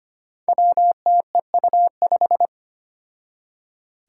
Morse code, heard as a keyed tone, about 25 wpm, 700 Hz. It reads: WTEU5
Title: Five Challenging Letter-Number Combinations 25wpm